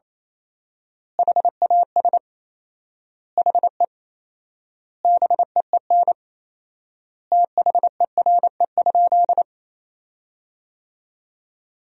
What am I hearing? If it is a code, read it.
Morse code, heard as a keyed tone, about 28 wpm, 700 Hz. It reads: HAS HE BEEN THERE?
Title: 4x Word-Spacing - Sentences from Top 100 Words - 28wpm